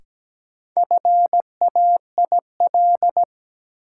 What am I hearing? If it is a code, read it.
Morse code, heard as a keyed tone, about 17 wpm, 700 Hz. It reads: FAIL
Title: Sentences from Top 1100 Words - New Word Drill Alt - 17wpm